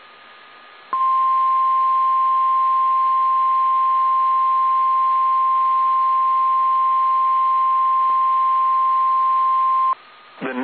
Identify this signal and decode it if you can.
Emergency Alert System (SAME) tones (the 1050 Hz attention tone), with speech. It is heard from 0.9 to 10.0 s.
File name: eas / KSDZKDJL / KSDZKDJL